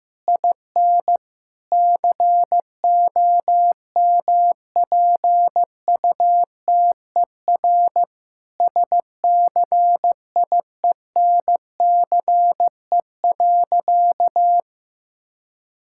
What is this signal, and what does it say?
Morse code, heard as a keyed tone, about 15 wpm, 700 Hz. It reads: IN COMPUTER SCIENCE.
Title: Sentences from Top 1000 Words - Review Part 6 - 15wpm